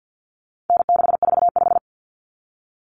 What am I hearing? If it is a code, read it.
Morse code, heard as a keyed tone, about 50 wpm, 700 Hz. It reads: N645
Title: Common Cut Numbers - 4-Digit 50wpm